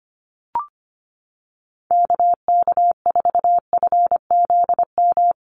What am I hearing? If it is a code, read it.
Morse code, heard as a keyed tone, about 25 wpm, 700 Hz. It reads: KK4FZM
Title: USA Call Signs Repeated 2x 25wpm